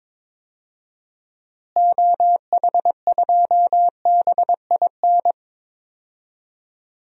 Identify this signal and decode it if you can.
Morse code, heard as a keyed tone, about 21 wpm, 700 Hz. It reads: OH2BIN
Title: DXCC-100 Call Signs - 6 Characters - Repeated 3x - 22wpm